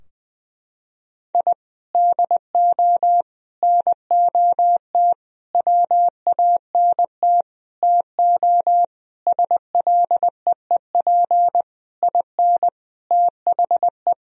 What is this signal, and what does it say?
Morse code, heard as a keyed tone, about 21 wpm, 700 Hz. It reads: I DO NOT WANT TO SLEEP IN THE
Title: Sentences from Top 1000 Words - Review Part 2 - 20wpm